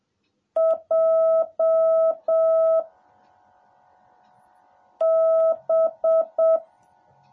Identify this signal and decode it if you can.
Morse code, heard as a keyed tone, about 7 wpm, 650 Hz. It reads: J B
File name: jh8cbh / 2021cw / cw18a